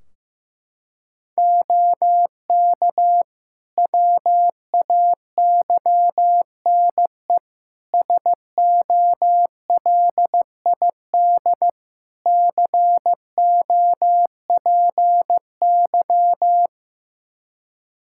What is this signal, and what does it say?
Morse code, heard as a keyed tone, about 16 wpm, 700 Hz. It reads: OK WAYNE SOLID COPY
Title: Practice QSO - Exchange 3 - More Info - 15wpm